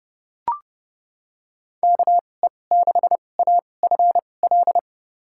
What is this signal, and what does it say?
Morse code, heard as a keyed tone, about 30 wpm, spaced as for 22 wpm, 700 Hz. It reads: KE6AFL
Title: USA Call Signs - 6 Characters - 2x Character Spacing - 30wpm